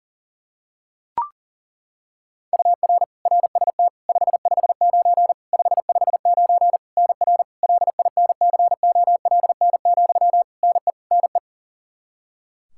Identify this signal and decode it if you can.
Morse code, heard as a keyed tone, about 40 wpm, 700 Hz. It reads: UR RST 559 559 NR LINCOLN, NE NE
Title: Practice QSO - Exchange 1 - Beginning - 40wpm